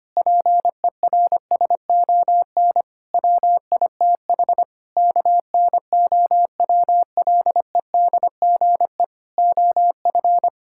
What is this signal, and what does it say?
Morse code, heard as a keyed tone, about 25 wpm, 700 Hz. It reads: PERSON WITH KNOWLEDGE OF
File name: practice / 25wpm / Sentences with 9-Word Phrases - 25wpm